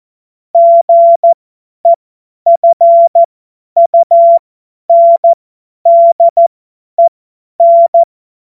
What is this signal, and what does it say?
Morse code, heard as a keyed tone, about 14 wpm, spaced as for 10 wpm, 680 Hz. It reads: GEFUNDEN
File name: Morsekurs / Witze / Witz 13 Februar 2024 50 F70